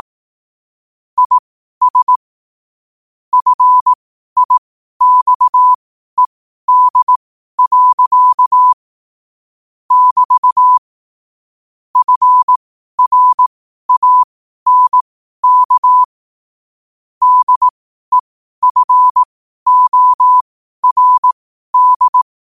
Morse code, heard as a keyed tone, about 18 wpm, spaced as for 12 wpm, 1000 Hz. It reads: IS FIXED. <BT> FRANK DEFORD
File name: qotdfiles / 12wpm